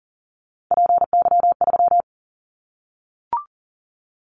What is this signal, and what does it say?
Morse code, heard as a keyed tone, about 40 wpm, 700 Hz. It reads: PY3
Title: Callsign Prefixes for Top 50 DXCC - 40wpm